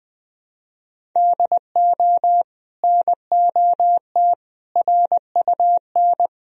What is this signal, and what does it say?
Morse code, heard as a keyed tone, about 20 wpm, 700 Hz. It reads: DO NOT RUN
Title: Sentences from Top 1000 Words - Review Part 1 - 20wpm